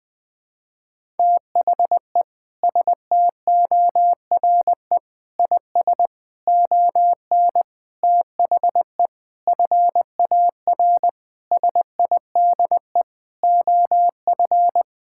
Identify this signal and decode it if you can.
Morse code, heard as a keyed tone, about 19 wpm, 700 Hz. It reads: THE STORE IS ON THE FAR SIDE OF
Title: Sentences with 6-Word Phrases - 20wpm